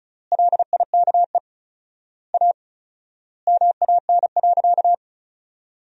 Morse code, heard as a keyed tone, about 34 wpm, 700 Hz. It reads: LIKE A MAN.
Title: 4x Word-Spacing - Sentences from Top 300 Words - 35wpm